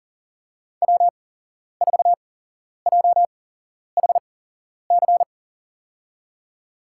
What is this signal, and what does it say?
Morse code, heard as a keyed tone, about 43 wpm, spaced as for 11 wpm, 700 Hz. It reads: W4JHC
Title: USA Call Signs - 5 Characters - 8x Character Spacing - 40wpm